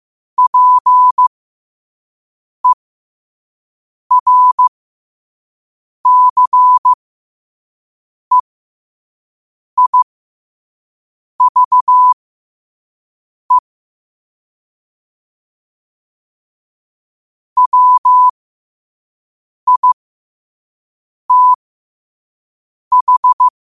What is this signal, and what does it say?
Morse code, heard as a keyed tone, about 15 wpm, spaced as for 5 wpm, 1000 Hz. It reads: PERCEIVE WITH
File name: qotdfiles / 5wpm